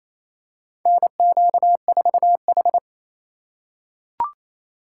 Morse code, heard as a keyed tone, about 28 wpm, 700 Hz. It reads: NQ4H
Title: USA Call Signs - 4 Characters - 2x Character Spacing - 28wpm